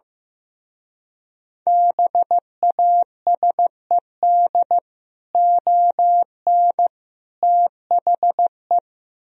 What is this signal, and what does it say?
Morse code, heard as a keyed tone, about 15 wpm, 700 Hz. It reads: BASED ON THE
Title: Speed-Racing - Sets of 3 Words 15wpm